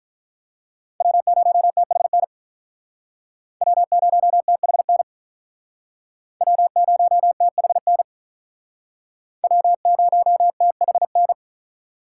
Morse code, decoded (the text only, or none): W0THN W0THN W0THN W0THN